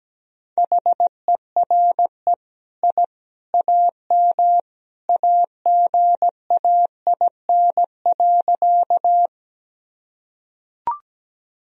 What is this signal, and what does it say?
Morse code, heard as a keyed tone, about 17 wpm, 700 Hz. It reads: HERE I AM AGAIN.
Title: Sentences from Top 1000 Words - Review Part 2 - 17wpm